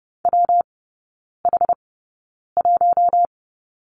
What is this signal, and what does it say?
Morse code, heard as a keyed tone, about 30 wpm, spaced as for 9 wpm, 700 Hz. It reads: WH1
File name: Lesson 19e - Letter w - Three Characters 30wpm